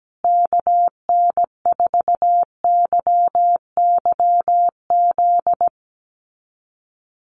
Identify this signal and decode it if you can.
Morse code, heard as a keyed tone, about 17 wpm, 700 Hz. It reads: KN4YYZ
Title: USA Call Signs Repeated 3x 17wpm